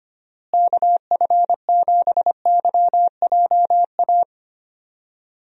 Morse code, heard as a keyed tone, about 25 wpm, 700 Hz. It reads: KF7YJA
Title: USA Call Signs - 6 Characters - Repeated 3x - 25wpm